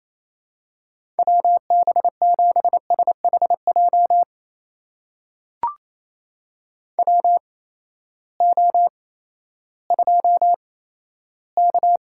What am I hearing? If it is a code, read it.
Morse code, heard as a keyed tone, about 27 wpm, spaced as for 14 wpm, 700 Hz. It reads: WB7SHJ WO2K
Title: USA Call Signs - 8x Character Spacing - 28wpm